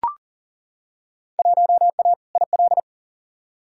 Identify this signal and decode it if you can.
Morse code, heard as a keyed tone, about 40 wpm, 700 Hz. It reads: 1A IL
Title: Field Day Exchanges Encore - 40wpm